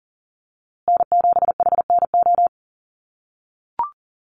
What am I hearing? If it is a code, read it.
Morse code, heard as a keyed tone, about 40 wpm, 700 Hz. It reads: N7HNO
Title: USA Call Signs Repeated 2x 40wpm